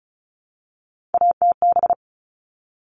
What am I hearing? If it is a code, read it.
Morse code, heard as a keyed tone, about 34 wpm, 700 Hz. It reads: ATB
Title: All Cut Numbers - 3-Digit 35wpm